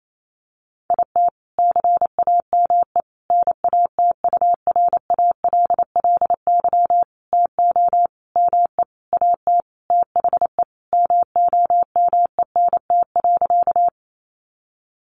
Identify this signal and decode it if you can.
Morse code, heard as a keyed tone, about 28 wpm, 700 Hz. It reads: IT CAME NATURALLY TO ME AT THE MOMENT.
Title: Sentences from Top 1300 Words 28wpm